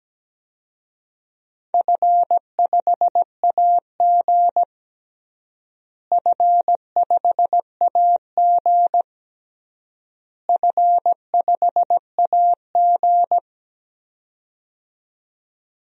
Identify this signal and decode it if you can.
Morse code, heard as a keyed tone, about 17 wpm, 700 Hz. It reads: F5AG F5AG F5AG
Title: DXCC-100 Call Signs - 4 Characters - Repeated 3x - 17wpm